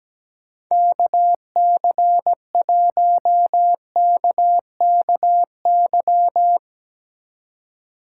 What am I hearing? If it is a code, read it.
Morse code, heard as a keyed tone, about 17 wpm, 700 Hz. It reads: KC1KKY
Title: Speed Racing - USA 2x3 Call Signs 17wpm